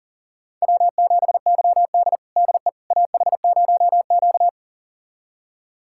Morse code, heard as a keyed tone, about 40 wpm, 700 Hz. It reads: W7YD DE AH0Q
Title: USA Callsign DE Callsign - 4 Characters - 40wpm